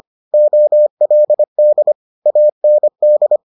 Morse code, heard as a keyed tone, about 25 wpm, 600 Hz. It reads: OLD AND